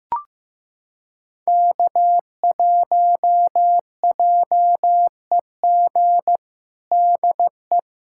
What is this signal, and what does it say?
Morse code, heard as a keyed tone, about 15 wpm, 700 Hz. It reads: K1JEG DE
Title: USA Callsign DE Callsign 15wpm